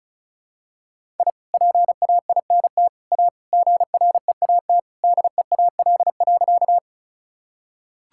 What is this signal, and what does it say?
Morse code, heard as a keyed tone, about 34 wpm, 700 Hz. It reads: I PAINT A GREAT DEAL.
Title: Sentences from Top 1000 Words - Review Part 6 - 35wpm